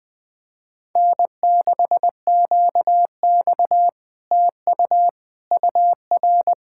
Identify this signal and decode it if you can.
Morse code, heard as a keyed tone, about 20 wpm, 700 Hz. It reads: N6QX TU UR